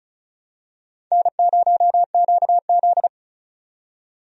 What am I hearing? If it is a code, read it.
Morse code, heard as a keyed tone, about 34 wpm, 700 Hz. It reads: N0QZ
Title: USA Call Signs - 4 Characters - 35wpm